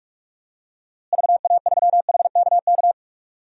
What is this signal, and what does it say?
Morse code, heard as a keyed tone, about 41 wpm, 700 Hz. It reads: VA3HKK
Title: DXCC-100 Call Signs - 6 Characters - Repeated 2x - 45wpm